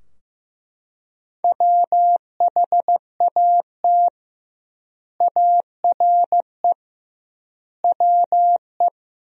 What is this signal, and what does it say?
Morse code, heard as a keyed tone, about 15 wpm, 700 Hz. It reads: WHAT ARE WE
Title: 2x Word-Spacing - Sentences from Top 100 Words - 15wpm